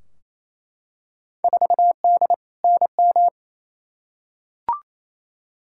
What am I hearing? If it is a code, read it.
Morse code, heard as a keyed tone, about 28 wpm, 700 Hz. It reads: 4D NM
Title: Field Day Exchanges Encore - 28wpm